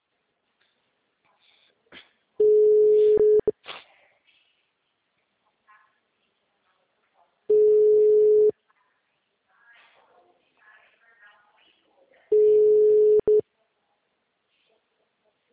Ringback tone (Europe, 425 Hz).